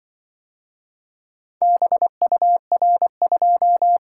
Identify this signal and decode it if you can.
Morse code, heard as a keyed tone, about 24 wpm, 700 Hz. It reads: BUR2
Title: Speed-Racing - Four Letter-Number Combinations - 20wpm